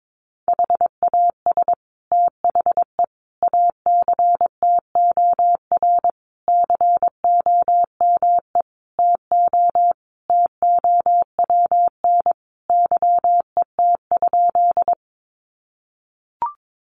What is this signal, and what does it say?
Morse code, heard as a keyed tone, about 22 wpm, 700 Hz. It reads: HAS THE ACTOR COME TO TOWN YET?
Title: Sentences from Top 1500 Words 22wpm